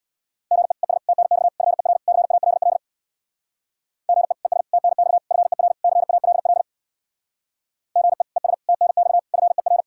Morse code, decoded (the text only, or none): GE ES TNX FER CALL GE ES TNX FER CALL GE ES TNX FER